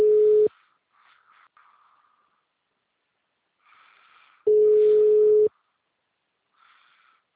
Ringback tone (Europe, 425 Hz).